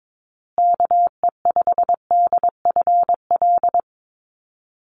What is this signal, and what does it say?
Morse code, heard as a keyed tone, about 22 wpm, 700 Hz. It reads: KE5DFL
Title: USA Call Signs Repeated 3x 22wpm